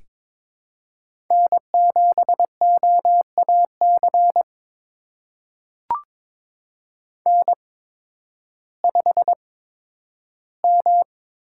Morse code, heard as a keyed tone, about 22 wpm, spaced as for 12 wpm, 700 Hz. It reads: N7OAC N5M
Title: USA Call Signs - 5 Characters - 8x Character Spacing - 22wpm